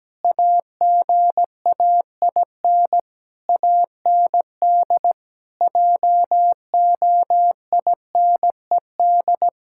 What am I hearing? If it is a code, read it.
Morse code, heard as a keyed tone, about 17 wpm, 700 Hz. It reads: AGAIN AND JOINED
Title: Pangram Challenge v2 - 17wpm